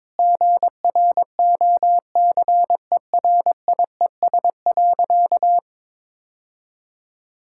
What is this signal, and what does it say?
Morse code, heard as a keyed tone, about 22 wpm, 700 Hz. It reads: GROCERIES.
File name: practice / 22wpm / Sentences with 3-Word Phrases - 22wpm